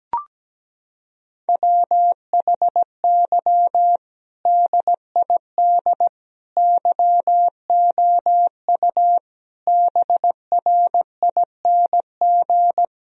Morse code, heard as a keyed tone, about 17 wpm, 700 Hz. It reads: WHY DID YOU BRING
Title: Sentences from Top 1000 Words - Review Part 3 - 17wpm